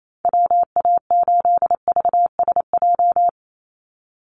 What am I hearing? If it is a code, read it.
Morse code, heard as a keyed tone, about 28 wpm, 700 Hz. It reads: WA8VSJ
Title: Call Signs - 28wpm